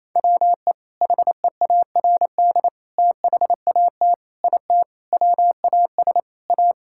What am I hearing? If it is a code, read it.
Morse code, heard as a keyed tone, about 28 wpm, 700 Hz. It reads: WE HEARD THAT IT WAS A
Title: Sentences from Top 1000 Words - Review Part 6 - 28wpm